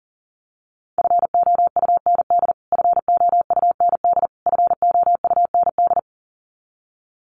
Morse code, heard as a keyed tone, about 40 wpm, 700 Hz. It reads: FOUND FOUND FOUND
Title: Top 300 Words Repeated 40wpm